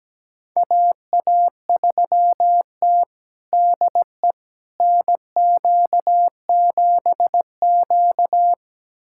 Morse code, heard as a keyed tone, about 17 wpm, 700 Hz. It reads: AA3T DE NQ7Q
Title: USA Callsign DE Callsign - 4 Characters - 17wpm